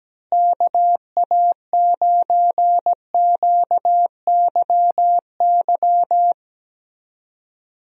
Morse code, heard as a keyed tone, about 17 wpm, 700 Hz. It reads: KA9QYY
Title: USA Call Signs - 6 Characters - Repeated 2x - 17wpm